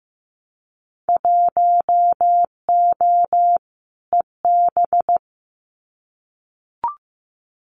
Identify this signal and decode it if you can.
Morse code, heard as a keyed tone, about 15 wpm, 700 Hz. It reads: 1O EB
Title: Winter Field Day Exchanges 15wpm